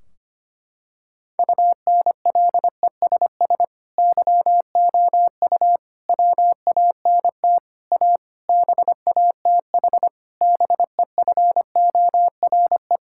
Morse code, heard as a keyed tone, about 25 wpm, 700 Hz. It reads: UNLESS YOU WANT A BATH BEFORE
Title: Sentences from Top 1600 Words 25wpm